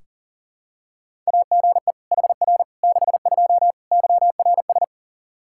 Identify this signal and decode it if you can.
Morse code, heard as a keyed tone, about 40 wpm, 700 Hz. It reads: AGE HR 62 YRS
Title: Practice QSO - Exchange 3 - More Info - 40wpm